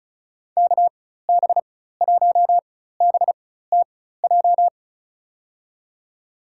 Morse code, heard as a keyed tone, about 36 wpm, spaced as for 16 wpm, 700 Hz. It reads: KB1BTJ
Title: USA Call Signs - 6 Characters - 4x Character Spacing - 35wpm